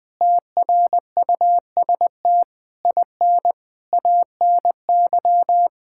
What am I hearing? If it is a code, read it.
Morse code, heard as a keyed tone, about 20 wpm, 700 Hz. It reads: TRUST IN ANY
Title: Difficult Sentences with Hints - 20wpm